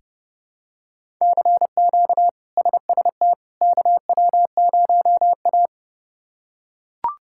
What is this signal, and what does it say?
Morse code, heard as a keyed tone, about 30 wpm, 700 Hz. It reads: CQ SST KW0A
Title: K1USN SST QSO Practice - 30wpm